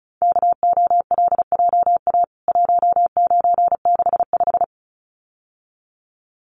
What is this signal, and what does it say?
Morse code, heard as a keyed tone, about 34 wpm, 700 Hz. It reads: KOLJA 1965
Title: CWT Exchange 35wpm